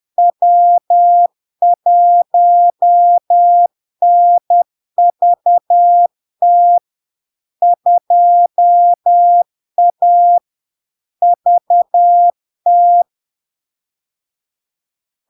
Morse code, heard as a keyed tone, about 10 wpm, 700 Hz. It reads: W1NVT 2A VT
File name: LARK-FD-2022-Log-in-CW-10wpm-700-Hz